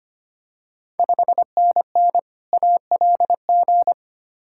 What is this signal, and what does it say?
Morse code, heard as a keyed tone, about 25 wpm, 700 Hz. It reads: 5NN ALG
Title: Ontario QSO Party Exchange 25wpm